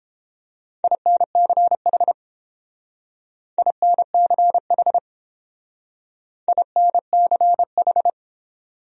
Morse code, heard as a keyed tone, about 30 wpm, 700 Hz. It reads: INCH INCH INCH